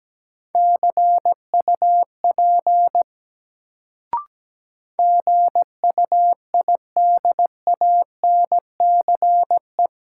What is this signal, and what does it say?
Morse code, heard as a keyed tone, about 17 wpm, 700 Hz. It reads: CUP GUIDANCE
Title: Sentences from Top 1600 Words - New Word Drill Alt - 17wpm